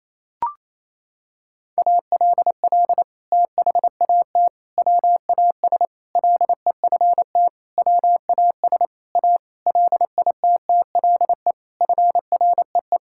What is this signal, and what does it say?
Morse code, heard as a keyed tone, about 27 wpm, 700 Hz. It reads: ALL THAT WAS LEFT WAS A LITTLE FREE
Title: Sentences from Top 1000 Words - Review Part 5 - 28wpm